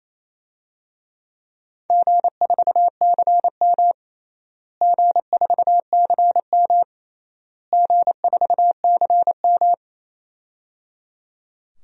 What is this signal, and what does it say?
Morse code, heard as a keyed tone, about 28 wpm, 700 Hz. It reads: G4CM G4CM G4CM
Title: DXCC-100 Call Signs - 4 Characters - Repeated 3x - 28wpm